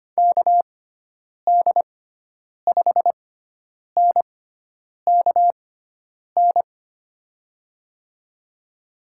Morse code, heard as a keyed tone, about 25 wpm, spaced as for 9 wpm, 700 Hz. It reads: KD5NKN
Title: USA Call Signs - 6 Characters - 6x Character Spacing - 25wpm